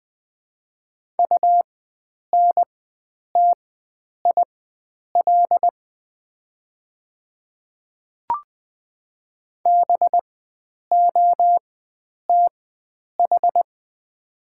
Morse code, heard as a keyed tone, about 20 wpm, spaced as for 9 wpm, 690 Hz. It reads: UNTIL BOTH
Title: 4x Character Spacing - Top 300 Words - 20wpm